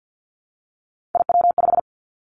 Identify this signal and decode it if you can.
Morse code, heard as a keyed tone, about 50 wpm, 700 Hz. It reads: IW5
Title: Three Challenging Letter-Number Combinations 50wpm